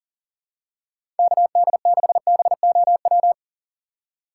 Morse code, heard as a keyed tone, about 40 wpm, 700 Hz. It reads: KD6BOW
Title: USA Call Signs - 6 Characters - 3x Character Spacing - 40wpm